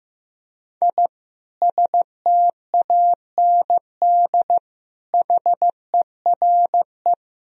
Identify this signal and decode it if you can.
Morse code, heard as a keyed tone, about 15 wpm, 700 Hz. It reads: I STAND HERE
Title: Sentences from Top 1000 Words - Review Part 3 - 15wpm